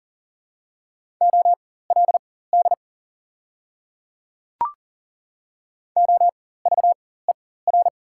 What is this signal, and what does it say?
Morse code, heard as a keyed tone, about 40 wpm, spaced as for 19 wpm, 690 Hz. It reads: OLD OVER
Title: 4x Character Spacing - Top 500 Words - 40wpm